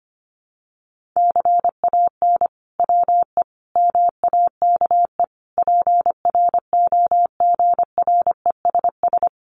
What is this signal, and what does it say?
Morse code, heard as a keyed tone, about 25 wpm, 700 Hz. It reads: CAN WE MAKE PROGRESS
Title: Sentences from Top 1500 Words 25wpm